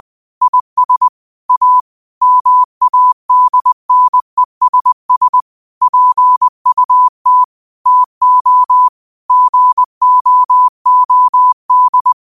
Morse code, heard as a keyed tone, about 20 wpm, 1000 Hz. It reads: IS A MADNESS PUT TO GOOD
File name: qotdfiles / 20wpm